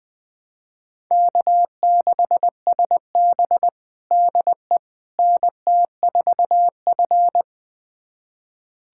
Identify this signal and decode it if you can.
Morse code, heard as a keyed tone, about 20 wpm, 700 Hz. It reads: K6SB DE NT4F
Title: USA Callsign DE Callsign - 4 Characters - 20wpm